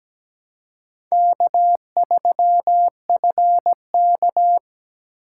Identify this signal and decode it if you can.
Morse code, heard as a keyed tone, about 20 wpm, 700 Hz. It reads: K3FK
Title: Speed Racing - USA 1x2 Call Signs 17wpm